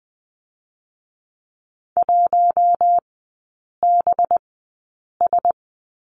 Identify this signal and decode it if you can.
Morse code, heard as a keyed tone, about 20 wpm, 700 Hz. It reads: 1 B S
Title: 218 Lesson 35e - Number 0 - Three Characters 20wpm